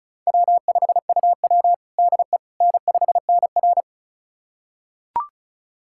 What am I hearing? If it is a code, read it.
Morse code, heard as a keyed tone, about 36 wpm, 700 Hz. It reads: W5UW DE N5NR
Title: USA Callsign DE Callsign - 4 Characters - 35wpm